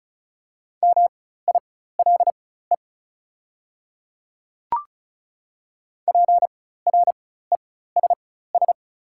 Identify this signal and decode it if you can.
Morse code, heard as a keyed tone, about 36 wpm, spaced as for 16 wpm, 690 Hz. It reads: MILE PRESS
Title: 4x Character Spacing - Top 300 Words - 35wpm